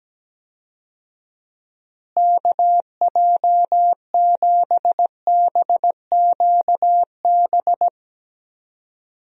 Morse code, heard as a keyed tone, about 17 wpm, 700 Hz. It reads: KJ7BQB
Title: USA Call Signs - 6 Characters - Repeated 3x - 17wpm